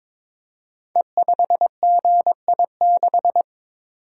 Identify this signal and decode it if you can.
Morse code, heard as a keyed tone, about 23 wpm, 700 Hz. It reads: E5GI6
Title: Code Groups - 5 Letter:Numbers - 22wpm